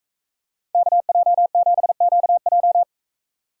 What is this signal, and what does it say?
Morse code, heard as a keyed tone, about 44 wpm, 700 Hz. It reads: KJ7QJ